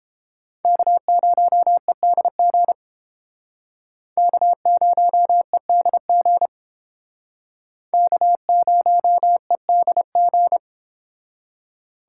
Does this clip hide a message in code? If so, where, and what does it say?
Yes: Morse code from 0.6 to 10.6 s, heard as a keyed tone, about 30 wpm, 700 Hz. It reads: K0EDG K0EDG K0EDG